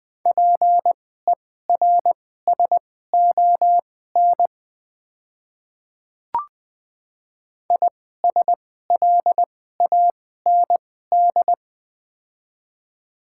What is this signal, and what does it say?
Morse code, heard as a keyed tone, about 21 wpm, spaced as for 15 wpm, 700 Hz. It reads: PERSON ISLAND
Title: 2x Character Spacing - Top 500 Words - 20wpm